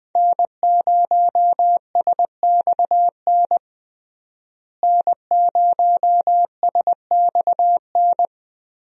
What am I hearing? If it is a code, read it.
Morse code, heard as a keyed tone, about 20 wpm, 700 Hz. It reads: N0SXN N0SXN
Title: USA Call Signs - 5 Characters - Repeated 2x - 20wpm